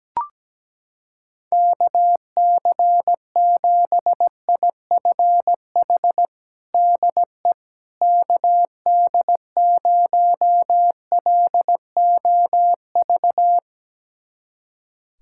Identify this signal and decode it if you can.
Morse code, heard as a keyed tone, about 17 wpm, 700 Hz. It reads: KC7IFH DE KD0LOV
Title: USA Callsign DE Callsign 17wpm